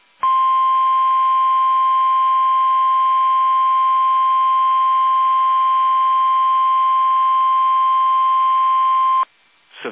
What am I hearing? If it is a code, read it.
Emergency Alert System (SAME) tones (the 1050 Hz attention tone). It is heard from 0.2 to 9.3 s.